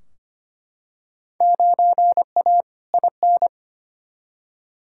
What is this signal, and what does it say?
Morse code, heard as a keyed tone, about 25 wpm, 700 Hz. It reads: 9A IN
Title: Field Day Exchanges Encore - 25wpm